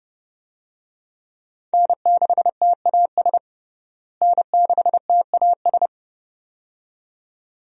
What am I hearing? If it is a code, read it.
Morse code, heard as a keyed tone, about 29 wpm, 700 Hz. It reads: N6TAS N6TAS
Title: USA Call Signs - 5 Characters - Repeated 2x - 30wpm